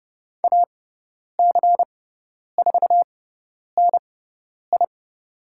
Morse code, heard as a keyed tone, about 30 wpm, spaced as for 10 wpm, 700 Hz. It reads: AC4NI
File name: Lesson 16g - Letter c - Call Signs 30wpm